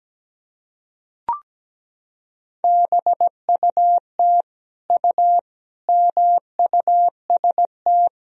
Morse code, heard as a keyed tone, about 17 wpm, 700 Hz. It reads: BUT U MUST
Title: Sentences from Top 300 Words and Abbreviations 17wpm